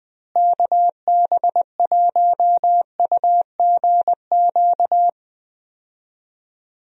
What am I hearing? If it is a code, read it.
Morse code, heard as a keyed tone, about 20 wpm, 700 Hz. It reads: KB1UGQ
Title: USA Call Signs - 6 Characters - Repeated 3x - 20wpm